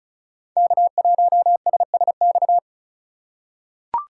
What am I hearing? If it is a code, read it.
Morse code, heard as a keyed tone, about 36 wpm, 700 Hz. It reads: K1SSX
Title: Code Groups - 5 Letter:Numbers - 35wpm